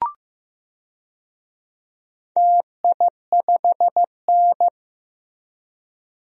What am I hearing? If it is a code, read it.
Morse code, heard as a keyed tone, about 15 wpm, 700 Hz. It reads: TI5N
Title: DXCC-100 Call Signs - 4 Characters - Repeated 2x - 15wpm